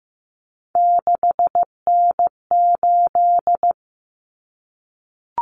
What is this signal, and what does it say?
Morse code, heard as a keyed tone, about 15 wpm, 700 Hz. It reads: 6N8
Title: Common Cut Numbers - 3-Digit 15wpm